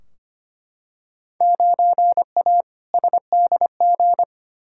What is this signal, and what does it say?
Morse code, heard as a keyed tone, about 25 wpm, 700 Hz. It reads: 9A SDG
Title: Field Day Exchanges Encore - 25wpm